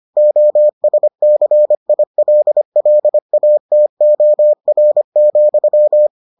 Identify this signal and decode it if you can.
Morse code, heard as a keyed tone, about 25 wpm, 600 Hz. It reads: OSCILLATOR,